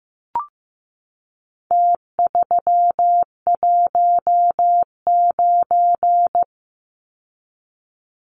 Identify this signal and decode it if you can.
Morse code, heard as a keyed tone, about 15 wpm, 700 Hz. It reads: T319
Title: Numbers - Contest Style - 4-Digit 15wpm